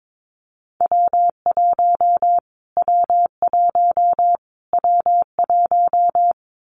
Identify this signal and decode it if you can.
Morse code, heard as a keyed tone, about 22 wpm, 700 Hz. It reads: W1 W1 W1
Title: Callsign Prefixes for Top 25 DXCC - Repeated 3x - 22wpm